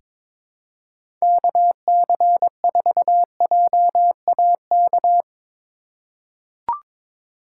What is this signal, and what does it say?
Morse code, heard as a keyed tone, about 22 wpm, 700 Hz. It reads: KC4JAK
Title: USA Call Signs - 6 Characters - 3x Character Spacing - 22wpm